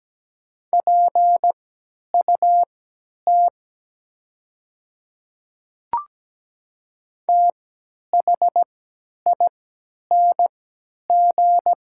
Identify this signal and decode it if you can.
Morse code, heard as a keyed tone, about 17 wpm, spaced as for 10 wpm, 700 Hz. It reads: PUT THING
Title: 3x Character Spacing - Top 300 Words - 17wpm